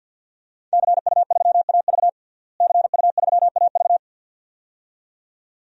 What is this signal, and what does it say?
Morse code, heard as a keyed tone, about 51 wpm, 700 Hz. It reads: CU3AV CU3AV
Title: DXCC-100 Call Signs - 5 Characters - Repeated 2x - 50wpm